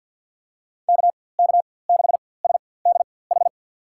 Morse code, heard as a keyed tone, about 50 wpm, spaced as for 23 wpm, 700 Hz. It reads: KK6SDH
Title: USA Call Signs - 4x Character Spacing - 50wpm